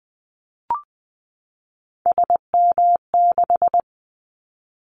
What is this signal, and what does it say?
Morse code, heard as a keyed tone, about 20 wpm, 700 Hz. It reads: SM6
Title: Callsign Prefixes for Top 25 DXCC - 20wpm